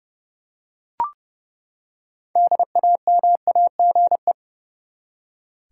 Morse code, heard as a keyed tone, about 29 wpm, 700 Hz. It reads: DAMAGE
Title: Six Letter Words - 30wpm